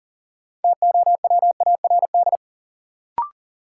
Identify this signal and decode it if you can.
Morse code, heard as a keyed tone, about 37 wpm, 700 Hz. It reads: TOWARD
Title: Sentences from Top 1000 Words - Review Part 6 - New Word Drill - 40wpm